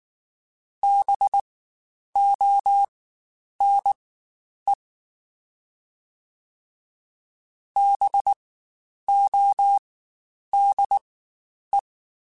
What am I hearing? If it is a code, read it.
Morse code, heard as a keyed tone, about 19 wpm, spaced as for 9 wpm, 800 Hz. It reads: BONE BODE
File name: Spaced Alphas / Spaced Alphas mp3s / Bs sp @18